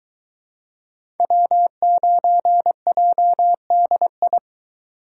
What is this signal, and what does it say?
Morse code, heard as a keyed tone, about 24 wpm, 700 Hz. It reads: W9JDI